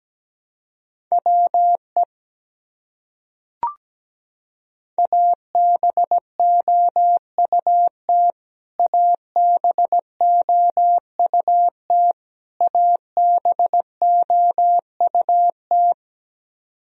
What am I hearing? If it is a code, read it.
Morse code, heard as a keyed tone, about 17 wpm, 700 Hz. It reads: WE ABOUT ABOUT ABOUT
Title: Top 100 Words Repeated 17wpm